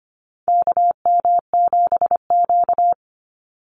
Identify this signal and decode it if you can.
Morse code, heard as a keyed tone, about 25 wpm, 700 Hz. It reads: KM7Q
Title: Call Signs - 25wpm